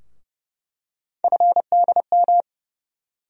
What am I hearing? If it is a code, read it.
Morse code, heard as a keyed tone, about 30 wpm, 700 Hz. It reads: FDM
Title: USA Call Sign Postfixes - 3 Letters 30wpm